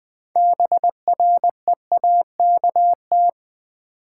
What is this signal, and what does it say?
Morse code, heard as a keyed tone, about 20 wpm, 700 Hz. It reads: BREAKT
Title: Build A Word - 6 Letters At A Time - 12-Letter Words - 20wpm